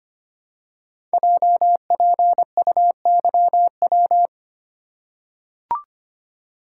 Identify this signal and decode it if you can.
Morse code, heard as a keyed tone, about 25 wpm, 700 Hz. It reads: JPUYW
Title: Code Groups - 5 Letters - 25wpm